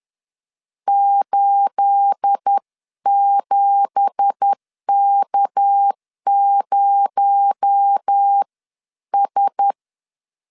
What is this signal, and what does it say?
Morse code, heard as a keyed tone, about 10 wpm, spaced as for 9 wpm, 800 Hz. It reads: 87K0S
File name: LEZIONI / 9 2867 10wpm 2733